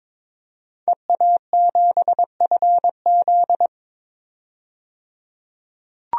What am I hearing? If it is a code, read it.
Morse code, heard as a keyed tone, about 22 wpm, 700 Hz. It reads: EA7FZ
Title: DXCC-100 Call Signs - 5 Characters - 22wpm